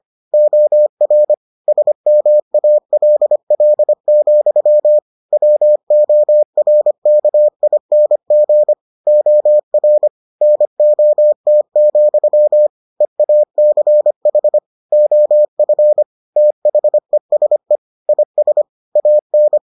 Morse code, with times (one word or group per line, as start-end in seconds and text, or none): 0.3-1.3 OR
1.7-5.0 SMALL,
5.3-8.7 WORKING
9.1-10.1 OR
10.4-12.7 NOT,
13.0-14.6 EACH
14.9-16.0 OF
16.4-17.8 THESE
18.1-18.6 IS
19.0-19.6 AN